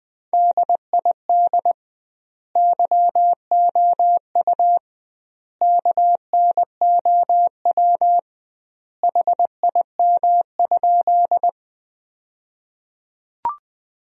Morse code, heard as a keyed tone, about 21 wpm, 700 Hz. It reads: DID YOU KNOW HIM?
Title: 2x Word-Spacing - Sentences from Top 100 Words - 20wpm